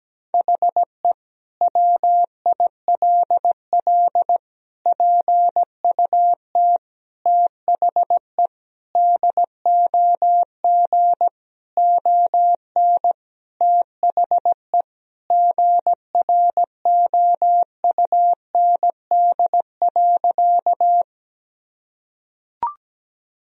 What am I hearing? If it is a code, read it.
Morse code, heard as a keyed tone, about 17 wpm, 700 Hz. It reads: HE WILL PUT THE DOG ON THE GROUND.
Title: Sentences from Top 1000 Words - Review Part 2 - 17wpm